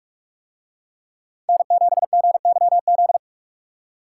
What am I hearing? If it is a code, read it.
Morse code, heard as a keyed tone, about 44 wpm, 700 Hz. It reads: N7GYZ